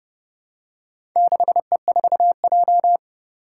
Morse code, heard as a keyed tone, about 30 wpm, 700 Hz. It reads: 6E4J
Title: Four Challenging Letter-Number Combinations 30wpm